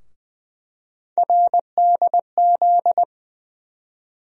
Morse code, heard as a keyed tone, about 20 wpm, 700 Hz. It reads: RDZ
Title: USA Call Sign Postfixes - 3 Letters 20wpm